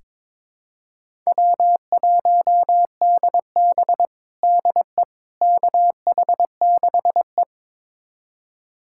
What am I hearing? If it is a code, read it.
Morse code, heard as a keyed tone, about 22 wpm, 700 Hz. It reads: W1DB DE KH6E
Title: USA Callsign DE Callsign - 4 Characters - 22wpm